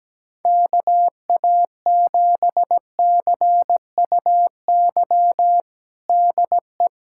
Morse code, heard as a keyed tone, about 17 wpm, 700 Hz. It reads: KA7CUY DE
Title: USA Callsign DE Callsign 17wpm